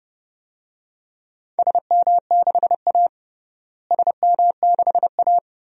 Morse code, heard as a keyed tone, about 30 wpm, 700 Hz. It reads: SM6A SM6A
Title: DXCC-100 Call Signs - 4 Characters - Repeated 3x - 30wpm